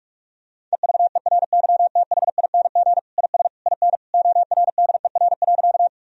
Morse code, heard as a keyed tone, about 46 wpm, 700 Hz. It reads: EVERYTHING IS IN ORDER.
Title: Sentences from Top 1000 Words - Review Part 2 - 45wpm